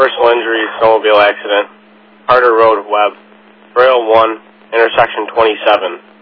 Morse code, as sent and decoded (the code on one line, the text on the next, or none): none